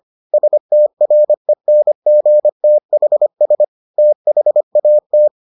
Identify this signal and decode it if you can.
Morse code, heard as a keyed tone, about 25 wpm, 600 Hz. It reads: STRENGTHS THAT